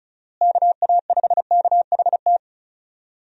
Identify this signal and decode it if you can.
Morse code, heard as a keyed tone, about 34 wpm, 700 Hz. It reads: KA5KHT
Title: USA Call Signs - 6 Characters - Repeated 3x - 35wpm